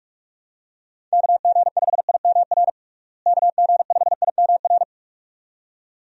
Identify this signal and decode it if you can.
Morse code, heard as a keyed tone, about 46 wpm, 700 Hz. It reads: KG5IMR KG5IMR
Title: USA Call Signs - 6 Characters - Repeated 2x - 45wpm